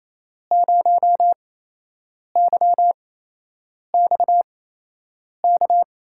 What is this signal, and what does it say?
Morse code, heard as a keyed tone, about 28 wpm, spaced as for 8 wpm, 700 Hz. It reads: 0YX K
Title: USA Call Signs - 6 Characters - 8x Character Spacing - 28wpm